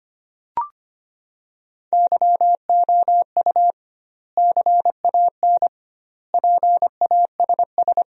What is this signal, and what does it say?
Morse code, heard as a keyed tone, about 25 wpm, 700 Hz. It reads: YOU CAN PASS
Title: 2x Word-Spacing - Sentences from Top 400 Words - 25wpm